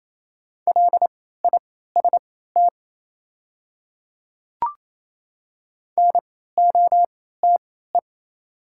Morse code, heard as a keyed tone, about 28 wpm, spaced as for 16 wpm, 690 Hz. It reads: LIST NOTE
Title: 3x Character Spacing - Top 500 Words - 28wpm